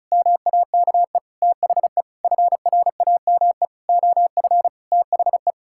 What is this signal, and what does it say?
Morse code, heard as a keyed tone, about 36 wpm, 700 Hz. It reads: MAKE THE FRAME OF THE
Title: Sentences from Top 1000 Words - Review Part 6 - 35wpm